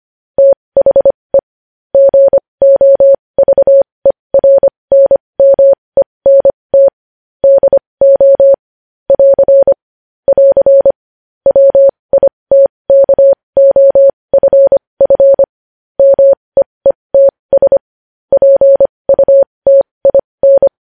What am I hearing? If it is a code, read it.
Morse code, heard as a keyed tone, about 25 wpm, spaced as for 20 wpm, 550 Hz. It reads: THE GOVERNMENT DO <AR> <AR> WITKOFF MEETS PUTIN
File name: podcast / News headlines-1x-25x20